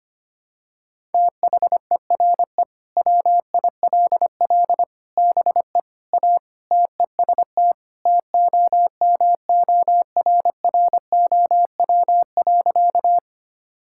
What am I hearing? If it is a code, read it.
Morse code, heard as a keyed tone, about 25 wpm, 700 Hz. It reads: THERE WILL BE A TEST TOMORROW.
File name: practice / 25wpm / Sentences with 3-Word Phrases - 25wpm